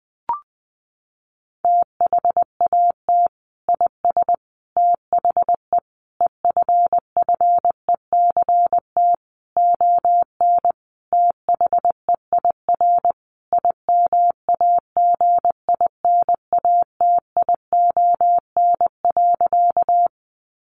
Morse code, heard as a keyed tone, about 19 wpm, 700 Hz. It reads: THAT IS THE EFFECT ON THEIR IMAGINATION.
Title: Sentences from Top 1300 Words 20wpm